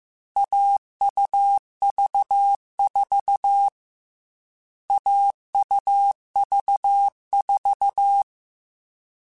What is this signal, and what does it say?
Morse code, heard as a keyed tone, about 15 wpm, 800 Hz. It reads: AUV4 AUV4
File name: Short Runs / Short Runs-mp3s / AUV AUV4 @13